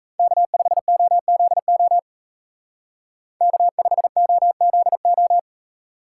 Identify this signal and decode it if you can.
Morse code, heard as a keyed tone, about 36 wpm, 700 Hz. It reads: K5OZO K5OZO